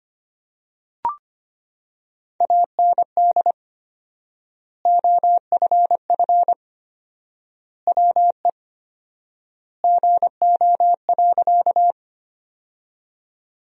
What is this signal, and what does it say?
Morse code, heard as a keyed tone, about 25 wpm, 700 Hz. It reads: AND OFF WE GO.
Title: 4x Word-Spacing - Sentences from Top 200 Words - 25wpm